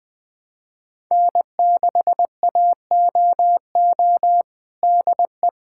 Morse code, heard as a keyed tone, about 20 wpm, 700 Hz. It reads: N6AOO DE
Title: USA Callsign DE Callsign - 5 Characters - 20wpm